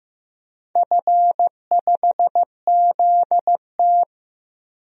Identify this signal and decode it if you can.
Morse code, heard as a keyed tone, about 15 wpm, 700 Hz. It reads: F5ZT
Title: DXCC-100 Call Signs - 4 Characters - 15wpm